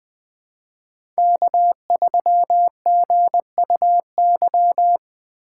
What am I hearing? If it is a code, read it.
Morse code, heard as a keyed tone, about 20 wpm, 700 Hz. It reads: K3GUY
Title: USA Call Signs - 2x Character Spacing - 20wpm